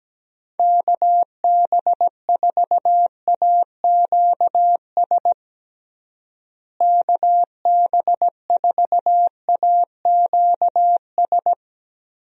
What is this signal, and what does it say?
Morse code, heard as a keyed tone, about 17 wpm, 700 Hz. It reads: KB4AQS KB4AQS
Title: USA Call Signs - 6 Characters - Repeated 2x - 17wpm